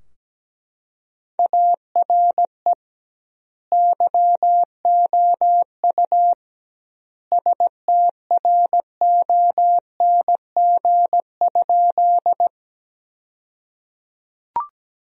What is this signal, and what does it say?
Morse code, heard as a keyed tone, about 17 wpm, 700 Hz. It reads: ARE YOU STRONG?
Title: 2x Word-Spacing - Sentences from Top 500 Words - 17wpm